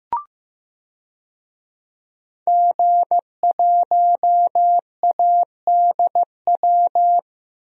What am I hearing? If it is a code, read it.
Morse code, heard as a keyed tone, about 15 wpm, 700 Hz. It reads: G1ADW
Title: DXCC-100 Call Signs - 5 Characters - Repeated 2x - 15wpm